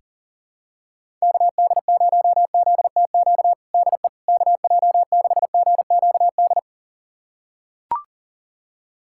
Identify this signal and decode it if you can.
Morse code, heard as a keyed tone, about 40 wpm, 700 Hz. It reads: KD0ZTQ DE KJ6GQD
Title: USA Callsign DE Callsign 40wpm